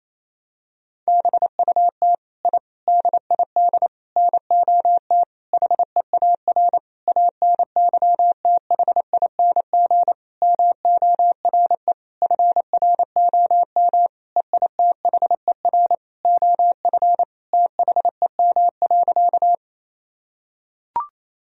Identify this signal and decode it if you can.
Morse code, heard as a keyed tone, about 28 wpm, 700 Hz. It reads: BUT I DID NOT HEAR ANYTHING MORE FROM EITHER OF THEM.
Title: Sentences from Top 1000 Words - Review Part 4 - 28wpm